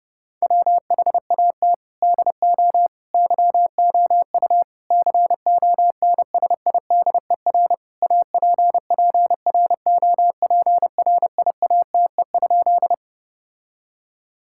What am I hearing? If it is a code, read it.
Morse code, heard as a keyed tone, about 32 wpm, 700 Hz. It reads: WHAT DO YOU CONSIDER APPROPRIATE?
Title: Sentences from Top 1000 Words - Review Part 5 - 30wpm